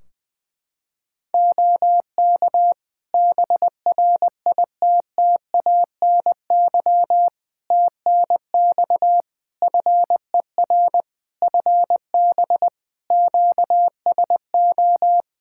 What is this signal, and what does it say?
Morse code, heard as a keyed tone, about 20 wpm, 700 Hz. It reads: OK BRITTANY TNX FER FB QSO
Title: K1USN SST QSO Practice 20wpm